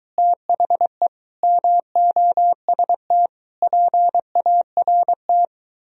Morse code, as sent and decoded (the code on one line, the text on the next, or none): - .... / -- --- ... - / .--. .- .-. -
THE MOST PART